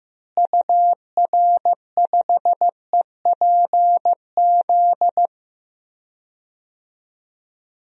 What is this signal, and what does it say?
Morse code, heard as a keyed tone, about 15 wpm, 700 Hz. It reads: UR5EPZ
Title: DXCC-100 Call Signs - 6 Characters - Repeated 2x - 15wpm